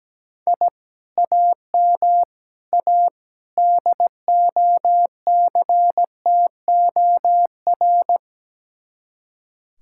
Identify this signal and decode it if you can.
Morse code, heard as a keyed tone, about 17 wpm, 700 Hz. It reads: I AM A DOCTOR
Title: Speed-Racing Common QSO Phrases Encore - 17wpm